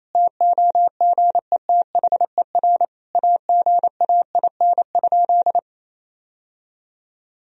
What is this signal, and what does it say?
Morse code, heard as a keyed tone, about 28 wpm, 700 Hz. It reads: TOGETHER AGAIN?
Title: Sentences from Top 1000 Words - Review Part 4 - 28wpm